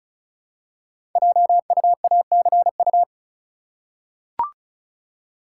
Morse code, heard as a keyed tone, about 36 wpm, 690 Hz. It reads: JUACU
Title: Code Groups - 5 Letters - 35wpm